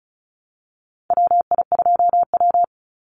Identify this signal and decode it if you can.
Morse code, heard as a keyed tone, about 34 wpm, 700 Hz. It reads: WI2W
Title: NA SOTA Chaser Callsigns 35wpm